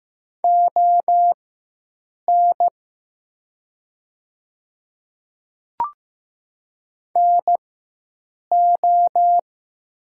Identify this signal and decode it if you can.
Morse code, heard as a keyed tone, about 15 wpm, spaced as for 7 wpm, 700 Hz. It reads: ON NO
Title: 4x Character Spacing - Top 100 Words - 15wpm